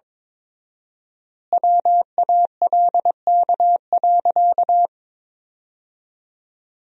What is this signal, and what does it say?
Morse code, heard as a keyed tone, about 22 wpm, 700 Hz. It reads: WALK.
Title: 4x Word-Spacing - Sentences from Top 300 Words - 22wpm